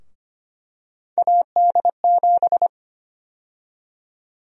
Morse code, heard as a keyed tone, about 25 wpm, 700 Hz. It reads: AD7
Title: USA Call Sign Prefixes - 25wpm